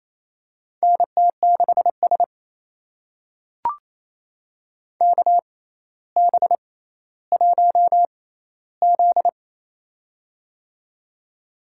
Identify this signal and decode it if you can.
Morse code, heard as a keyed tone, about 27 wpm, spaced as for 14 wpm, 700 Hz. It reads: NT6S KB1Z
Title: USA Call Signs - 4 Characters - 6x Character Spacing - 28wpm